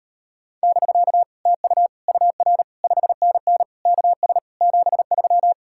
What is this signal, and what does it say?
Morse code, heard as a keyed tone, about 38 wpm, 700 Hz. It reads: <BK> TU UR 5NN KS 73